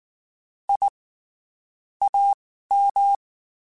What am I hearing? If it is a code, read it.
Morse code, heard as a keyed tone, about 19 wpm, spaced as for 14 wpm, 800 Hz. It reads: I AM
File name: (spaced) / QSOs Spaced-mp3s / WD0FBT 1 @18 sp